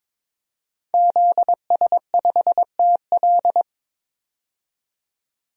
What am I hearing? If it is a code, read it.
Morse code, heard as a keyed tone, about 22 wpm, 700 Hz. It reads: ZS5TL
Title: DXCC-100 Call Signs - 5 Characters - 22wpm